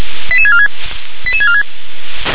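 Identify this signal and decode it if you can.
Five-tone selective call (ZVEI), heard as quick runs of five tones, tones of about 70 ms. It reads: 88646 88646
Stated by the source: ZVEI five-tone call received by SDR.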